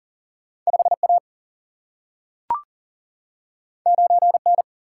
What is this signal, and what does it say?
Morse code, heard as a keyed tone, about 40 wpm, 700 Hz. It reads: 5A 9N
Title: Single-Letter Single-Number 40wpm